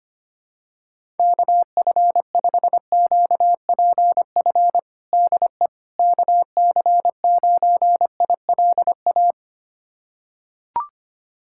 Morse code, heard as a keyed tone, about 25 wpm, 700 Hz. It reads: KF5QPF DE KC9ILA
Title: USA Callsign DE Callsign 25wpm